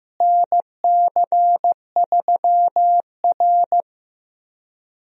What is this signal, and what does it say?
Morse code, heard as a keyed tone, about 15 wpm, 700 Hz. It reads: NC3R
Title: USA Call Signs - 4 Characters - Repeated 2x - 15wpm